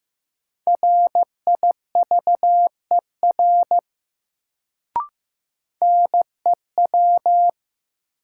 Morse code, heard as a keyed tone, about 15 wpm, 700 Hz. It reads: RIVER NEW
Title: Sentences from Top 300 Words - New Word Drill - 15wpm